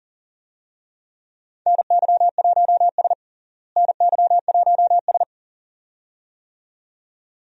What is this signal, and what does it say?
Morse code, heard as a keyed tone, about 40 wpm, 700 Hz. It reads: NY1S NY1S
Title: USA Call Signs - 4 Characters - Repeated 2x - 40wpm